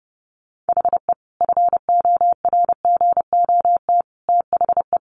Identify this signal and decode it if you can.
Morse code, heard as a keyed tone, about 28 wpm, 700 Hz. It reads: HE FORGOT THE
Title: Sentences from Top 1400 Words 30wpm